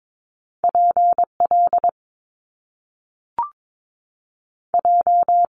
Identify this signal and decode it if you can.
Morse code, heard as a keyed tone, about 22 wpm, 700 Hz. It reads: PL J
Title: Top Two Letter Combinations in Words 22wpm